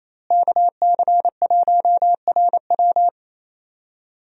Morse code, heard as a keyed tone, about 28 wpm, 700 Hz. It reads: KC1RW
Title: USA Call Signs - 5 Characters - Repeated 2x - 28wpm